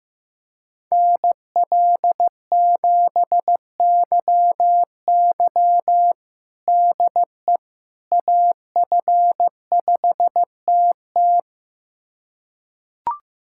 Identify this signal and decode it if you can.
Morse code, heard as a keyed tone, about 16 wpm, 700 Hz. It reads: NL7YY DE AF5TT
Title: USA Callsign DE Callsign - 5 Characters - 15wpm